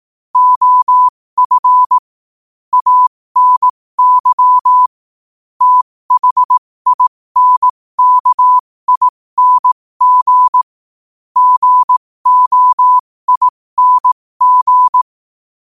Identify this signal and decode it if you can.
Morse code, heard as a keyed tone, about 18 wpm, spaced as for 15 wpm, 1000 Hz. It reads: OF ANY THINKING GOING
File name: qotdfiles / 15wpm